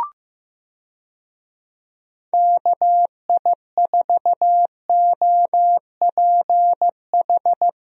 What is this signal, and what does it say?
Morse code, heard as a keyed tone, about 15 wpm, 700 Hz. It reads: KI4OPH
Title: USA Call Signs - 6 Characters - Repeated 2x - 15wpm